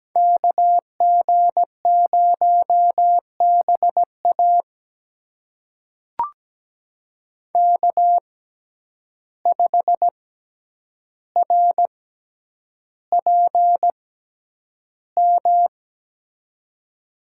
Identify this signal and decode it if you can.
Morse code, heard as a keyed tone, about 17 wpm, spaced as for 9 wpm, 700 Hz. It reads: KG0BA K5RPM
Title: USA Call Signs - 5 Characters - 6x Character Spacing - 17wpm